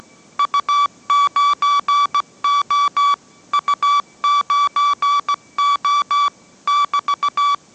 Morse code, heard as a keyed tone, about 17 wpm, 1150 Hz. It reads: U9O U9O <BT>
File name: mrx / mrx080 18wpm